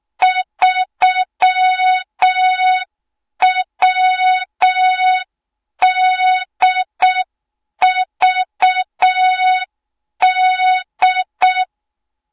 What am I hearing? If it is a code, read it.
Morse code, heard as a keyed tone, about 6 wpm, 3700 Hz. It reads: 3WDVD
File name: Morze-HA2MN / hang / 20 GYAKORLAT